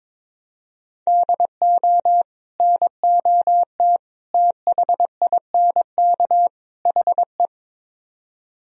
Morse code, heard as a keyed tone, about 22 wpm, 700 Hz. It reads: DO NOT THINK HE
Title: Speed-Racing - Sets of 4 Words 22wpm